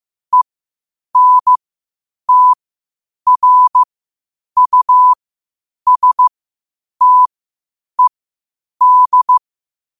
Morse code, heard as a keyed tone, about 15 wpm, spaced as for 8 wpm, 1000 Hz. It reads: ENTRUSTED
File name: qotdfiles / 8wpm